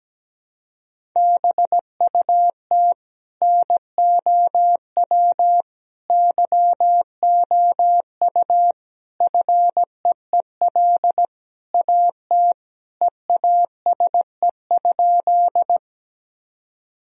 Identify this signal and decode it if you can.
Morse code, heard as a keyed tone, about 17 wpm, 700 Hz. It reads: BUT NOW YOU FEEL AT EASE?
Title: Sentences from Top 1000 Words - Review Part 1 - 17wpm